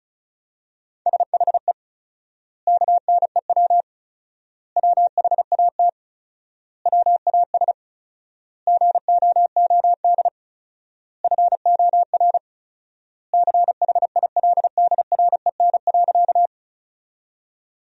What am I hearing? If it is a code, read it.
Morse code, heard as a keyed tone, about 36 wpm, 700 Hz. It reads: SHE KNEW WHAT WAS GOOD FOR CHILDREN.
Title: 4x Word-Spacing - Sentences from Top 400 Words - 35wpm